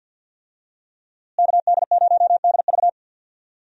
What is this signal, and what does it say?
Morse code, heard as a keyed tone, about 52 wpm, 700 Hz. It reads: KD0DV
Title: USA Call Signs - 5 Characters - Repeated 3x - 50wpm